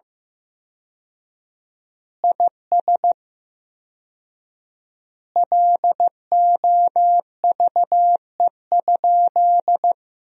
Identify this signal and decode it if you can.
Morse code, heard as a keyed tone, about 15 wpm, 700 Hz. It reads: IS LOVE?
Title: 4x Word-Spacing - Sentences from Top 400 Words - 15wpm